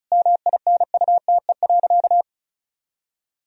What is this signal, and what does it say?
Morse code, heard as a keyed tone, about 36 wpm, 700 Hz. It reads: MINUTE.
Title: Sentences from Top 1000 Words - Review Part 4 - 35wpm